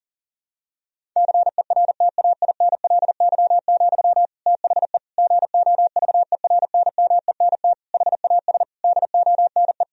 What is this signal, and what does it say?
Morse code, heard as a keyed tone, about 37 wpm, 700 Hz. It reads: CERTAINLY, THE GOVERNMENT HAS DONE
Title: Sentences from Top 1000 Words - Review Part 3 - 40wpm